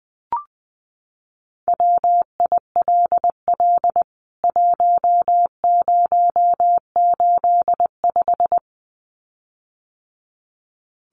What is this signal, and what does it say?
Morse code, heard as a keyed tone, about 20 wpm, 700 Hz. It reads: WILL 1085
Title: CWT Exchange 20wpm